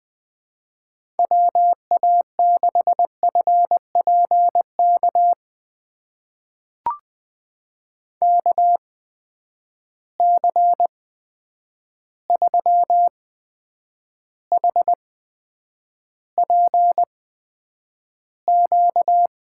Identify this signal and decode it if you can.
Morse code, heard as a keyed tone, about 20 wpm, spaced as for 9 wpm, 700 Hz. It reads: WA6FPK KC3HPQ
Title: USA Call Signs - 6 Characters - 8x Character Spacing - 20wpm